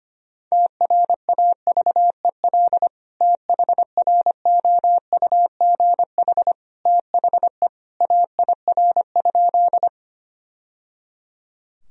Morse code, heard as a keyed tone, about 25 wpm, 700 Hz. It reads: TRAVEL THROUGH THE AIR?
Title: Sentences from Top 1000 Words - Review Part 2 - 25wpm